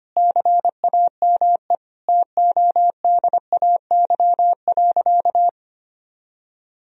Morse code, heard as a keyed tone, about 25 wpm, 700 Hz. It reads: CAME TODAY.
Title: Sentences from Top 1000 Words - Review Part 3 - 25wpm